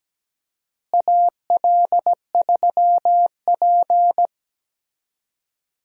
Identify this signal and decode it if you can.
Morse code, heard as a keyed tone, about 17 wpm, 700 Hz. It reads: AL3P
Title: USA Call Signs - 4 Characters - Repeated 3x - 17wpm